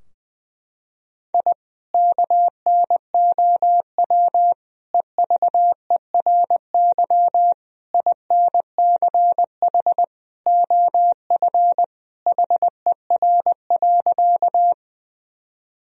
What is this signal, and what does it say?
Morse code, heard as a keyed tone, about 20 wpm, 700 Hz. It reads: I KNOW EVERY INCH OF HER.
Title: Sentences from Top 1000 Words - Review Part 3 - 20wpm